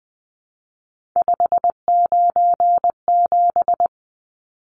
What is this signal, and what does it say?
Morse code, heard as a keyed tone, about 20 wpm, 700 Hz. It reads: 597
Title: Numbers - 3-Digit 20wpm